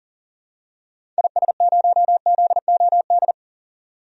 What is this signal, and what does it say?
Morse code, heard as a keyed tone, about 37 wpm, 700 Hz. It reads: IS0ZOD
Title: DXCC-100 Call Signs - 6 Characters - 40wpm